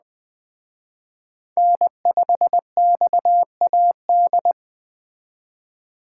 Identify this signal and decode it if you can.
Morse code, heard as a keyed tone, about 21 wpm, 700 Hz. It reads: N5XAD